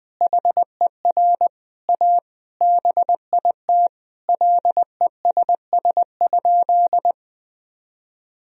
Morse code, heard as a keyed tone, about 21 wpm, 700 Hz. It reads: HER A BIT LESS?
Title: Sentences from Top 1000 Words - Review Part 2 - 20wpm